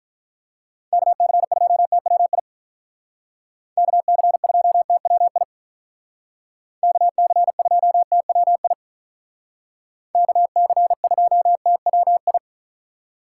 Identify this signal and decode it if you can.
Morse code, heard as a keyed tone, about 44 wpm, 700 Hz. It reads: KC2TWI KC2TWI KC2TWI KC2TWI